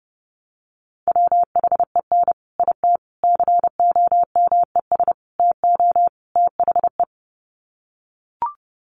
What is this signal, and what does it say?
Morse code, heard as a keyed tone, about 30 wpm, 700 Hz. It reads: WHEN IT COMES TO THE
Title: Sets of 5 Words 30wpm